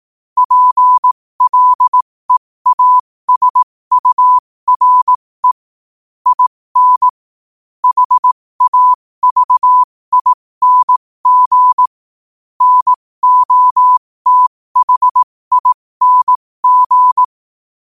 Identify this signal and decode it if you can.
Morse code, heard as a keyed tone, about 18 wpm, spaced as for 15 wpm, 1000 Hz. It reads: PLEASURE IN HAVING NOTHING
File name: qotdfiles / 15wpm